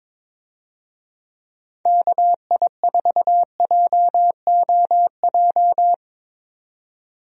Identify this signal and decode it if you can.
Morse code, heard as a keyed tone, about 22 wpm, 700 Hz. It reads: KI4JOJ
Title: USA Call Signs - 6 Characters - 22wpm